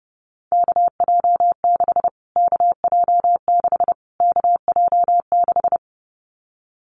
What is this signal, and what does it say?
Morse code, heard as a keyed tone, about 28 wpm, 690 Hz. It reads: KJ6 KJ6 KJ6
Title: Callsign Prefixes for Top 25 DXCC - Repeated 3x - 30wpm